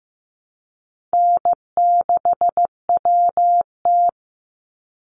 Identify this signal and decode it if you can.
Morse code, heard as a keyed tone, about 15 wpm, 700 Hz. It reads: N6WT
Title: NA SOTA Chaser Callsigns 15wpm